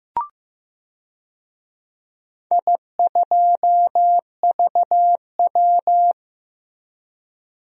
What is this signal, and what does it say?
Morse code, heard as a keyed tone, about 15 wpm, 700 Hz. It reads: I2VW
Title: DXCC-100 Call Signs - 4 Characters - Repeated 3x - 15wpm